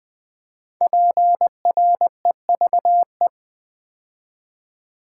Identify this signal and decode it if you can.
Morse code, heard as a keyed tone, about 20 wpm, 700 Hz. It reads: PREVE
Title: Build A Word - 5 Letters At A Time - 10-Letter Words - 20wpm